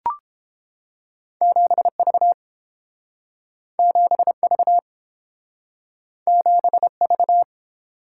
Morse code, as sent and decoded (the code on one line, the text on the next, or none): --... ...- / --... ...- / --... ...-
7V 7V 7V